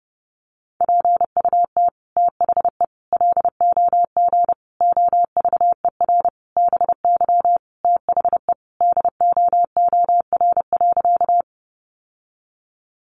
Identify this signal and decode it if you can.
Morse code, heard as a keyed tone, about 30 wpm, 700 Hz. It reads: PUT THE LOG OVER BY THE DOOR.
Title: Sentences from Top 1400 Words 30wpm